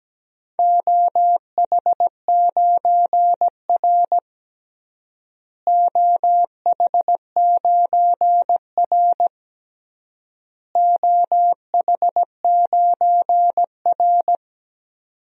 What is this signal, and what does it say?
Morse code, heard as a keyed tone, about 17 wpm, 700 Hz. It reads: OH9R OH9R OH9R
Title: DXCC-100 Call Signs - 4 Characters - Repeated 3x - 17wpm